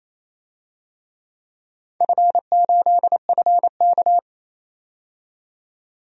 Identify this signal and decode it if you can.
Morse code, heard as a keyed tone, about 28 wpm, 700 Hz. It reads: F8FK
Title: DXCC-100 Call Signs - 4 Characters - 28wpm